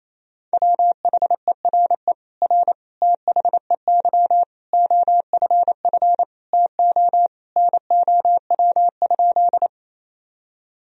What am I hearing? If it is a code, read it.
Morse code, heard as a keyed tone, about 28 wpm, 700 Hz. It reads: WHERE R THEY OFF TO NOW?
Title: Sentences from Top 200 Words and Abbreviations 28wpm